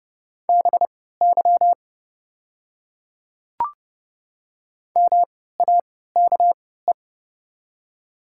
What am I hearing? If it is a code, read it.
Morse code, heard as a keyed tone, about 30 wpm, spaced as for 17 wpm, 690 Hz. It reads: BY MAKE
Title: 3x Character Spacing - Top 100 Words - 30wpm